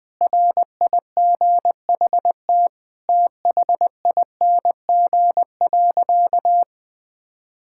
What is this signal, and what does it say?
Morse code, heard as a keyed tone, about 20 wpm, 700 Hz. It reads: RIGHT THING.
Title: Sentences with 5-Word Phrases - Spoken Beginning - 20wpm